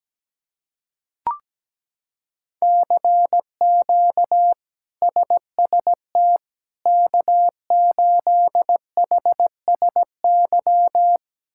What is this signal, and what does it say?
Morse code, heard as a keyed tone, about 17 wpm, 700 Hz. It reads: CQ SST K8HSY
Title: K1USN SST QSO Practice - 17wpm